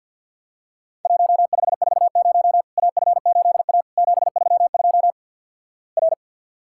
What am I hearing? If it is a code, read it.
Morse code, heard as a keyed tone, about 50 wpm, 700 Hz. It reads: 1540 AF8A 732 R
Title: Worked All Europe DX Contest - QTC Exchange with Repeats - 50wpm